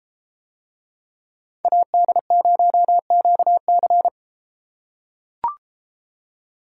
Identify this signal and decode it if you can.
Morse code, heard as a keyed tone, about 33 wpm, 700 Hz. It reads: AD0QC